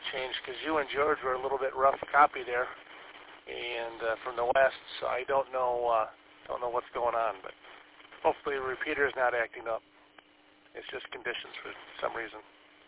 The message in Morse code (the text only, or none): none